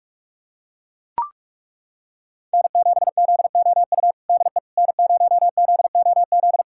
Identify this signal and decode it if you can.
Morse code, heard as a keyed tone, about 41 wpm, 700 Hz. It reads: N7ZOU DE N0ZOZ
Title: USA Callsign DE Callsign - 5 Characters - 45wpm